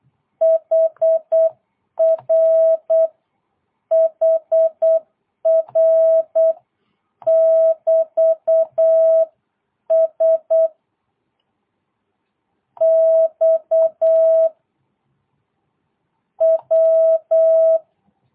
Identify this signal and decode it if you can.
Morse code, heard as a keyed tone, about 8 wpm, 660 Hz. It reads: HR HR <BT> S X W
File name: jh8cbh / 2022cw